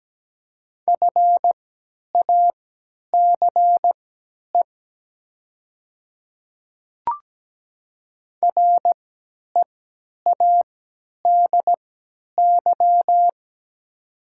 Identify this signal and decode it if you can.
Morse code, heard as a keyed tone, about 17 wpm, spaced as for 10 wpm, 700 Hz. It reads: FACE READY
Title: 3x Character Spacing - Top 500 Words - 17wpm